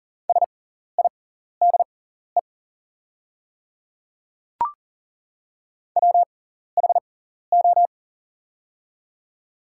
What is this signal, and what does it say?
Morse code, heard as a keyed tone, about 40 wpm, spaced as for 14 wpm, 700 Hz. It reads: SIDE WHO
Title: 6x Character Spacing - Top 100 Words - 40wpm